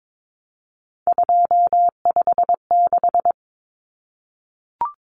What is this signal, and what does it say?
Morse code, heard as a keyed tone, about 22 wpm, 700 Hz. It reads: 256
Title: Common Cut Numbers - 3-Digit 22wpm